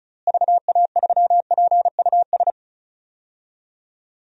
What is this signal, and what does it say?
Morse code, heard as a keyed tone, about 34 wpm, 700 Hz. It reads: VA3PUS
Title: DXCC-100 Call Signs - 6 Characters - Repeated 2x - 35wpm